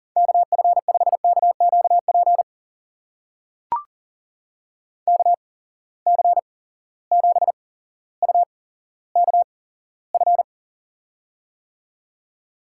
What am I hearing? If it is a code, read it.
Morse code, heard as a keyed tone, about 40 wpm, spaced as for 17 wpm, 700 Hz. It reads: KF5KQP KC7UKF
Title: USA Call Signs - 6 Characters - 8x Character Spacing - 40wpm